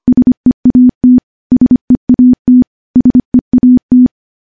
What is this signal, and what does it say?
Morse code, heard as a keyed tone, about 25 wpm, 250 Hz. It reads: SEAT SEAT SEAT
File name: morse / morse-training-25WPM